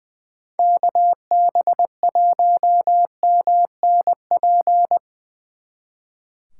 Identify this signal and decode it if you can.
Morse code, heard as a keyed tone, about 22 wpm, 700 Hz. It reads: KB1MNP
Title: Speed Racing - USA 2x3 Call Signs 20wpm